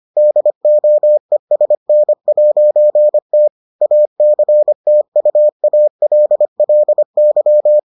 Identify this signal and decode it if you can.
Morse code, heard as a keyed tone, about 25 wpm, 600 Hz. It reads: DOESN'T ACTUALLY